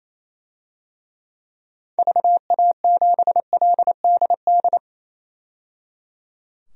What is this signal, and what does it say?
Morse code, heard as a keyed tone, about 28 wpm, 700 Hz. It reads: VA7LDD
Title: DXCC-100 Call Signs - 6 Characters - 28wpm